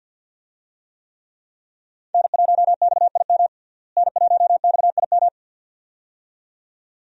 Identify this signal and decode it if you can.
Morse code, heard as a keyed tone, about 48 wpm, 700 Hz. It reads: N1XIM N1XIM
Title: USA Call Signs - 5 Characters - Repeated 2x - 50wpm